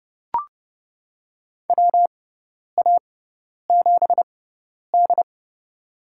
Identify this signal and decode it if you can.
Morse code, heard as a keyed tone, about 29 wpm, spaced as for 10 wpm, 700 Hz. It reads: WA7D
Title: USA Call Signs - 4 Characters - 6x Character Spacing - 30wpm